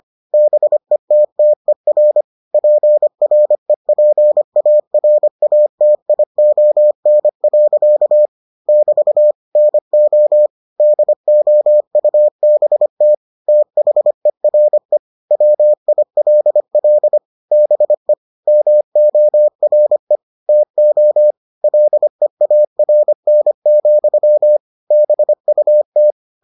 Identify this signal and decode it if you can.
Morse code, heard as a keyed tone, about 25 wpm, 600 Hz. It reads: BETTER PREPARATION. <BT> NO DOUBT THERE WILL BE MORE TO LEARN, BUT